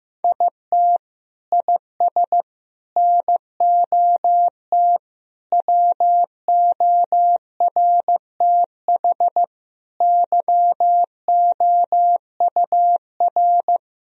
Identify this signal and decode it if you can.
Morse code, heard as a keyed tone, about 15 wpm, 700 Hz. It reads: IT IS NOT WORTH YOUR
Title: Sentences from Top 1000 Words - Review Part 1 - 15wpm